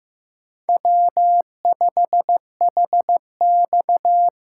Morse code, heard as a keyed tone, about 15 wpm, 700 Hz. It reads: W5HX
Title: USA Call Signs - 4 Characters - 3x Character Spacing - 15wpm